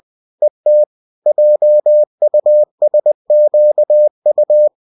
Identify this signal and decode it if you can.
Morse code, heard as a keyed tone, about 20 wpm, 600 Hz. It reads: ET JUSQU